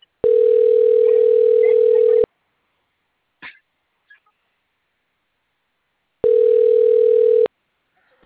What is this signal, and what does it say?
Ringback tone (North America), 440 + 480 Hz.